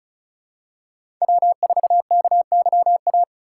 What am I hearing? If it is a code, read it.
Morse code, heard as a keyed tone, about 34 wpm, 700 Hz. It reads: W4KYA
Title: USA Call Signs - 5 Characters - 4x Character Spacing - 35wpm